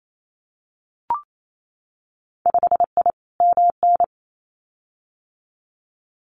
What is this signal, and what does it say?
Morse code, heard as a keyed tone, about 28 wpm, 700 Hz. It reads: 5I MN
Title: Winter Field Day Exchanges Encore 28wpm